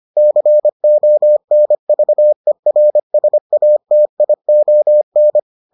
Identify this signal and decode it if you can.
Morse code, heard as a keyed tone, about 25 wpm, 600 Hz. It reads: CONVERSATION